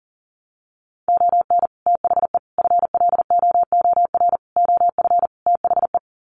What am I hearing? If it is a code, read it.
Morse code, heard as a keyed tone, about 39 wpm, 700 Hz. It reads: ON THE FLOOR OF THE
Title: Sets of 5 Words 40wpm